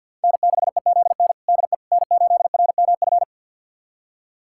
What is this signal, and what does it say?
Morse code, heard as a keyed tone, about 52 wpm, 700 Hz. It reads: N6EZN DE N8RMF
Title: USA Callsign DE Callsign - 5 Characters - 50wpm